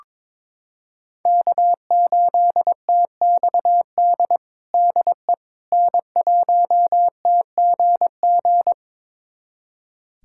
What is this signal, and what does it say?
Morse code, heard as a keyed tone, about 21 wpm, 700 Hz. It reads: K8TXD DE N1TGG
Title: USA Callsign DE Callsign - 5 Characters - 22wpm